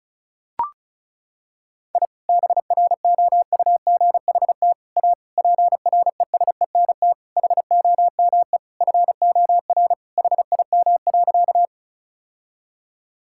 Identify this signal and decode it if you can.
Morse code, heard as a keyed tone, about 36 wpm, 700 Hz. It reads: I BROUGHT A PRESENT HOME FOR HIM.
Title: Sentences from Top 1000 Words - Review Part 5 - 35wpm